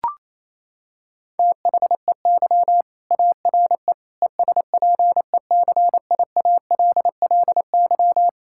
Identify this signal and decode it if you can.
Morse code, heard as a keyed tone, about 27 wpm, 700 Hz. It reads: THEY ARE ESPECIALLY
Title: Sentences from Top 1000 Words - Review Part 5 - 28wpm